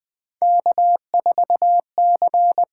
Morse code, heard as a keyed tone, about 20 wpm, 700 Hz. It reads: K4C
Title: USA Call Signs - 5 Characters - 8x Character Spacing - 20wpm